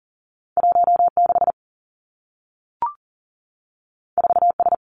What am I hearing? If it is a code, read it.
Morse code, heard as a keyed tone, about 40 wpm, 700 Hz. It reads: 1 6 4 S
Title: Two Challenging Letter-Number Combinations 40wpm